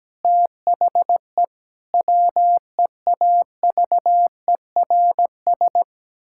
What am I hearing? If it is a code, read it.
Morse code, heard as a keyed tone, about 17 wpm, 700 Hz. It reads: THE WEAVERS
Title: Pangram Challenge v2 - 17wpm